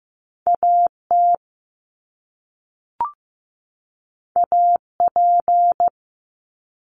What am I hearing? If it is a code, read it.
Morse code, heard as a keyed tone, about 15 wpm, 700 Hz. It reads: AT AP
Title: Top Two Letter Combinations in Words 15wpm